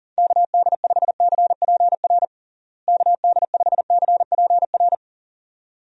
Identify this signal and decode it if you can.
Morse code, heard as a keyed tone, about 40 wpm, 700 Hz. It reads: KD5CPR KD5CPR
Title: USA Call Signs - 6 Characters - Repeated 2x - 40wpm